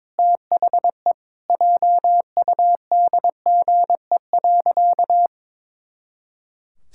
Morse code, heard as a keyed tone, about 22 wpm, 700 Hz. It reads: THE JUDGE.
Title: Pangram Challenge v2 - 22wpm